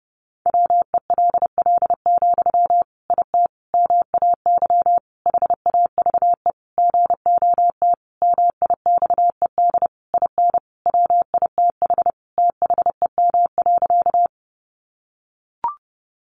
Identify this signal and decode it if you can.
Morse code, heard as a keyed tone, about 30 wpm, 700 Hz. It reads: WELL, IT MAY HAVE GOT MIXED IN WITH THEM.
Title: Sentences from Top 1500 Words 30wpm